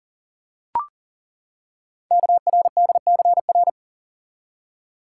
Morse code, heard as a keyed tone, about 42 wpm, 700 Hz. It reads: KRDCR
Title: Code Groups - 5 Letters - 40wpm